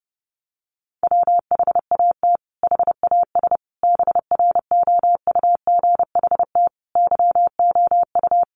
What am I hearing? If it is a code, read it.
Morse code, heard as a keyed tone, about 28 wpm, 700 Hz. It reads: WHAT HAS BROUGHT YOU
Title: Sentences from Top 1500 Words 30wpm